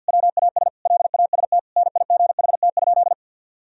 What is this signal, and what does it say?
Morse code, heard as a keyed tone, about 52 wpm, 700 Hz. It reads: WAS LAST NIGHT?
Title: Sentences from Top 1000 Words - Review Part 1 - 50wpm